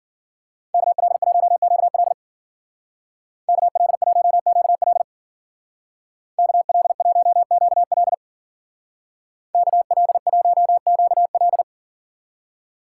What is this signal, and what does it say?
Morse code, heard as a keyed tone, about 48 wpm, 700 Hz. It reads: KL1QL KL1QL KL1QL KL1QL
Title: Speed Racing - USA 2x2 Call Signs 40wpm